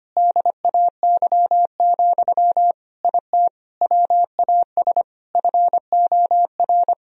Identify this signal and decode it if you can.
Morse code, heard as a keyed tone, about 25 wpm, 700 Hz. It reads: DAY, IT WAS FOR
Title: Sentences from Top 1000 Words - Review Part 3 - 25wpm